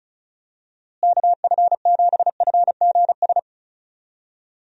Morse code, heard as a keyed tone, about 36 wpm, 700 Hz. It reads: KF7FGS
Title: USA Call Signs - 6 Characters - Repeated 3x - 35wpm